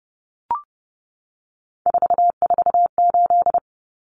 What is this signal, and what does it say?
Morse code, heard as a keyed tone, about 30 wpm, 700 Hz. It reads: 448
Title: Numbers - 3-Digit 30wpm